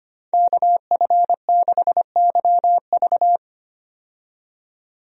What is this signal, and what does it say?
Morse code, heard as a keyed tone, about 25 wpm, 700 Hz. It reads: KF6YV
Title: USA Call Signs - 5 Characters - 25wpm